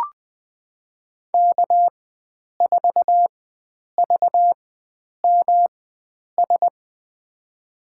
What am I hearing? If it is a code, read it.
Morse code, heard as a keyed tone, about 20 wpm, spaced as for 9 wpm, 700 Hz. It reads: K4VMS
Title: USA Call Signs - 5 Characters - 4x Character Spacing - 20wpm